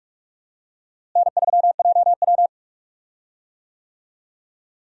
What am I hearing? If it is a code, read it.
Morse code, heard as a keyed tone, about 43 wpm, 700 Hz. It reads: N3JW
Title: USA Call Signs - 4 Characters - Repeated 3x - 45wpm